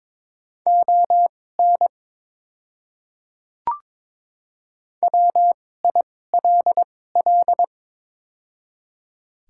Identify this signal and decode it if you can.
Morse code, heard as a keyed tone, about 23 wpm, spaced as for 16 wpm, 690 Hz. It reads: ON WILL
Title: 2x Character Spacing - Top 200 Words - 22wpm